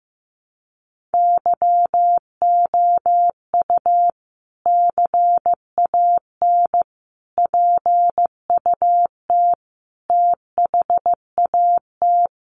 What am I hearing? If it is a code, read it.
Morse code, heard as a keyed tone, about 15 wpm, 700 Hz. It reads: YOU CAN PUT THAT
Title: Sentences from Top 1500 Words 15wpm